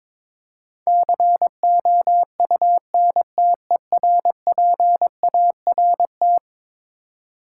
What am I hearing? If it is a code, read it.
Morse code, heard as a keyed tone, about 21 wpm, 700 Hz. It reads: COUNTERPART
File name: practice / 22wpm / Eleven Letter Words - 22wpm